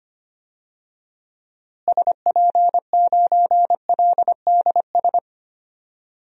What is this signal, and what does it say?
Morse code, heard as a keyed tone, about 25 wpm, 700 Hz. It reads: SP9LDS
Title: DXCC-100 Call Signs - 6 Characters - Repeated 2x - 25wpm